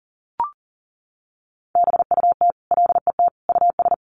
Morse code, heard as a keyed tone, about 40 wpm, 700 Hz. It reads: BUT LET US
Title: Sentences from Top 1500 Words 40wpm